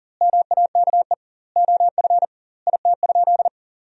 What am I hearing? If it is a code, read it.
Morse code, heard as a keyed tone, about 40 wpm, 700 Hz. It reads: MAKE OF IT?
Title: 2x Word-Spacing - Sentences from Top 100 Words - 40wpm